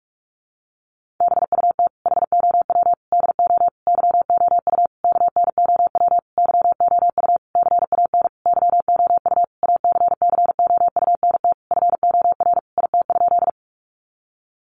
Morse code, heard as a keyed tone, about 44 wpm, 700 Hz. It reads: BUT HOW DO YOU KNOW YOU CAN YOU ACCOUNT FOR IT?
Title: Sentences from Top 900 Words 45wpm